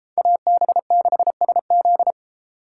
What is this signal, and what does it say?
Morse code, heard as a keyed tone, about 28 wpm, 700 Hz. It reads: AB6SZ